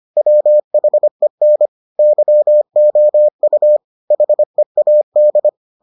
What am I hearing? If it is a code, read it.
Morse code, heard as a keyed tone, about 25 wpm, 600 Hz. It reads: WHEN YOU HEAD